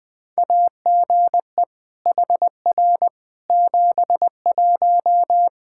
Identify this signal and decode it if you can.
Morse code, heard as a keyed tone, about 20 wpm, 700 Hz. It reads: AGE HR 71
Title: Practice QSO - Exchange 3 - More Info - 20wpm